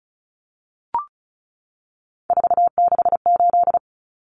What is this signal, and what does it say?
Morse code, heard as a keyed tone, about 34 wpm, 700 Hz. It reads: 468
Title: Numbers - 3-Digit 35wpm